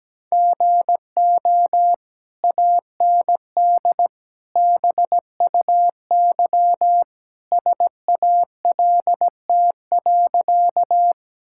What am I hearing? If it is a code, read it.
Morse code, heard as a keyed tone, about 17 wpm, 700 Hz. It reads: GO AND BUY SALT.
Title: Sentences from Top 1000 Words - Review Part 5 - 17wpm